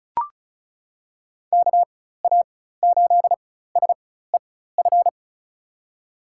Morse code, heard as a keyed tone, about 36 wpm, spaced as for 16 wpm, 700 Hz. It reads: KA8SEF
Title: USA Call Signs - 6 Characters - 4x Character Spacing - 35wpm